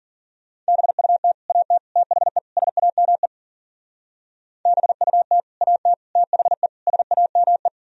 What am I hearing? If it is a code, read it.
Morse code, heard as a keyed tone, about 50 wpm, 700 Hz. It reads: BUT AT THE SAME BUT AT THE SAME
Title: Speed-Racing - Sets of 4 Words 40wpm